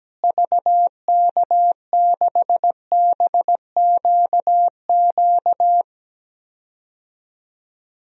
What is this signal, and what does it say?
Morse code, heard as a keyed tone, about 17 wpm, 700 Hz. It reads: VK6BQQ
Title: DXCC-100 Call Signs - 6 Characters - Repeated 2x - 17wpm